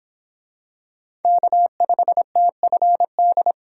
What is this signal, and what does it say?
Morse code, heard as a keyed tone, about 24 wpm, 700 Hz. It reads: K5TFD